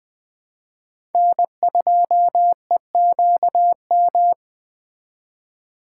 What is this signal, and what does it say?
Morse code, heard as a keyed tone, about 23 wpm, 700 Hz. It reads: N2EQM
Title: Speed Racing - USA 1x3 Call Signs 20wpm